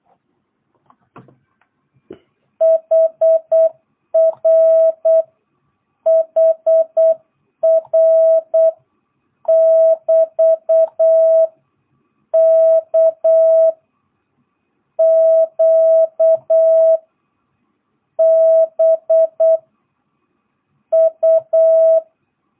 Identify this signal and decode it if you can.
Morse code, heard as a keyed tone, about 8 wpm, 650 Hz. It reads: HR HR <BT> K Q B U
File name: jh8cbh / 2022cw